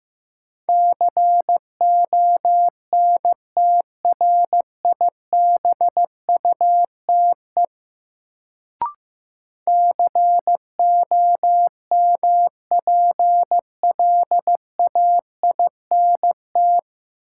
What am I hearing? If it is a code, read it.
Morse code, heard as a keyed tone, about 15 wpm, 700 Hz. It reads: CONTRIBUTE COMPLAINT
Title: Sentences from Top 1400 Words - New Word Drill Alt - 15wpm